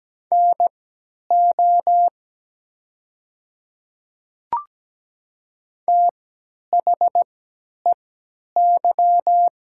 Morse code, heard as a keyed tone, about 17 wpm, spaced as for 10 wpm, 700 Hz. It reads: NO THEY
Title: 3x Character Spacing - Top 200 Words - 17wpm